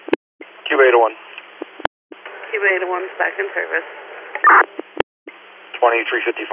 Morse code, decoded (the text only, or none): none